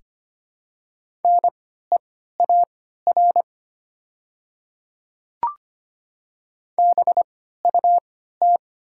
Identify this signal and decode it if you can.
Morse code, heard as a keyed tone, about 25 wpm, spaced as for 14 wpm, 700 Hz. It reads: NEAR BUT
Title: 3x Character Spacing - Top 200 Words - 25wpm